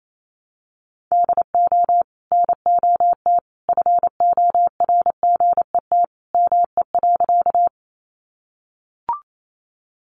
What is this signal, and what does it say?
Morse code, heard as a keyed tone, about 28 wpm, 700 Hz. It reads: DO NOT FORGET ME.
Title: Sentences from Top 1200 Words 28wpm